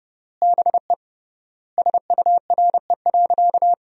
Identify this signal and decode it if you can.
Morse code, heard as a keyed tone, about 28 wpm, 700 Hz. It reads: BE SURE.
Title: 3x Word-Spacing - Sentences from Top 500 Words - 30wpm